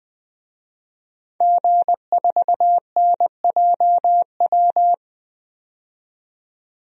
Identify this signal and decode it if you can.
Morse code, heard as a keyed tone, about 20 wpm, 700 Hz. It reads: G4NJW
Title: DXCC-100 Call Signs - 5 Characters - 20wpm